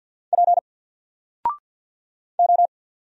Morse code, heard as a keyed tone, about 48 wpm, 700 Hz. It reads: P O
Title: Single Character (Letter:Number:Punctuation:Prosign) 50wpm